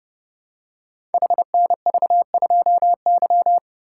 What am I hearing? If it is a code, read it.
Morse code, heard as a keyed tone, about 30 wpm, 700 Hz. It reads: HNV2Y
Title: Code Groups - 5 Letter:Numbers - 30wpm